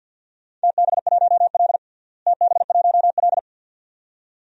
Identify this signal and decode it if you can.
Morse code, heard as a keyed tone, about 48 wpm, 700 Hz. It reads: TB1L TB1L
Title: DXCC-100 Call Signs - 4 Characters - Repeated 2x - 50wpm